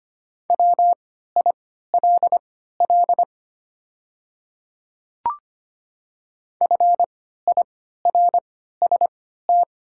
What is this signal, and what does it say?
Morse code, heard as a keyed tone, about 25 wpm, spaced as for 14 wpm, 700 Hz. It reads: WILL FIRST
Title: 3x Character Spacing - Top 100 Words - 25wpm